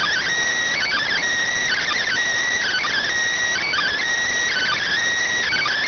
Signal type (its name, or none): Emergency Alert System (SAME) tones